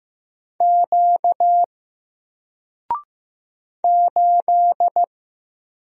Morse code, heard as a keyed tone, about 15 wpm, 700 Hz. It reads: Q 8
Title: Single Letter-Number 15wpm